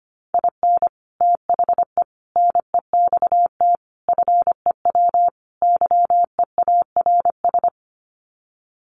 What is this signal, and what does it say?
Morse code, heard as a keyed tone, about 25 wpm, 700 Hz. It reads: IN THE NEXT FEW YEARS
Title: Sets of 5 Words Encore 25wpm